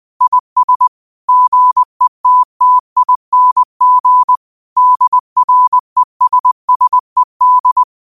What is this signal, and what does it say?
Morse code, heard as a keyed tone, about 20 wpm, 1000 Hz. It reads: IS GETTING DRESSED
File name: 20wpm